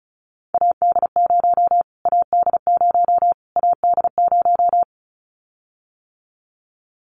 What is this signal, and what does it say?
Morse code, heard as a keyed tone, about 34 wpm, 700 Hz. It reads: AD0 AD0 AD0
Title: Callsign Prefixes for Top 25 DXCC - Repeated 3x - 35wpm